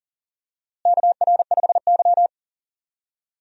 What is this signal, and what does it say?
Morse code, heard as a keyed tone, about 40 wpm, 700 Hz. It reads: KR5Y
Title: USA Call Signs - 4 Characters - Repeated 2x - 40wpm